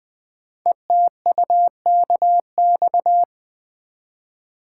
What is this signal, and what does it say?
Morse code, heard as a keyed tone, about 20 wpm, 700 Hz. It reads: ETUKX
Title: Code Groups - 5 Letters - 20wpm